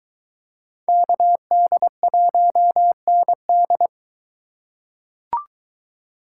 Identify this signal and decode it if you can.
Morse code, heard as a keyed tone, about 23 wpm, 700 Hz. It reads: KD1ND